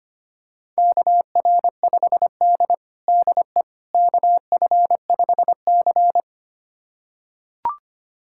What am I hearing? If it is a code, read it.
Morse code, heard as a keyed tone, about 25 wpm, 700 Hz. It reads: KR5D DE KF5C
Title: USA Callsign DE Callsign - 4 Characters - 25wpm